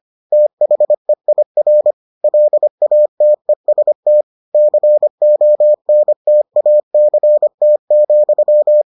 Morse code, heard as a keyed tone, about 25 wpm, 600 Hz. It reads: THEIR LATEST CONTACT,